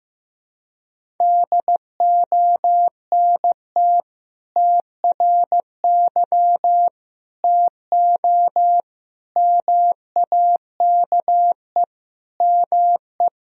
Morse code, heard as a keyed tone, about 15 wpm, 700 Hz. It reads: DONT TRY TO MAKE ME
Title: Sentences from Top 1000 Words - Review Part 1 - 15wpm